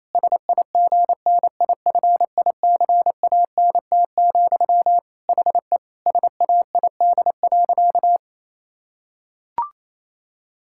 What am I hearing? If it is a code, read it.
Morse code, heard as a keyed tone, about 28 wpm, 700 Hz. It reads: SIGNIFICANT, HE SAID.
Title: Sentences from Top 1000 Words - Review Part 6 - 28wpm